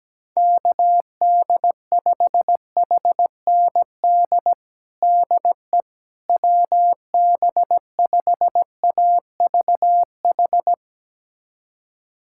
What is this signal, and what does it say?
Morse code, heard as a keyed tone, about 17 wpm, 700 Hz. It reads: KD5HND DE WB5AVH
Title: USA Callsign DE Callsign - 6 Characters - 17wpm